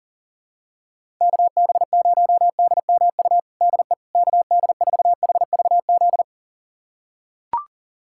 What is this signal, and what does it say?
Morse code, heard as a keyed tone, about 40 wpm, 700 Hz. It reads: KB0DMU DE KD4HVZ
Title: USA Callsign DE Callsign 40wpm